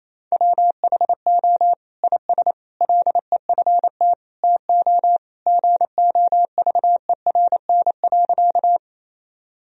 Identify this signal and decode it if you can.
Morse code, heard as a keyed tone, about 28 wpm, 690 Hz. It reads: WHO IS LEFT TO GOVERN.
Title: Sentences from Top 1000 Words - Review Part 2 - 28wpm